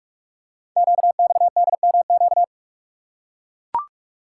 Morse code, heard as a keyed tone, about 44 wpm, 700 Hz. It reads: QXDMQ
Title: Code Groups - 5 Letters - 45wpm